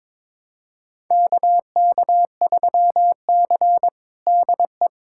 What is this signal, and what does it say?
Morse code, heard as a keyed tone, about 22 wpm, 700 Hz. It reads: KK3C DE
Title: USA Callsign DE Callsign - 4 Characters - 22wpm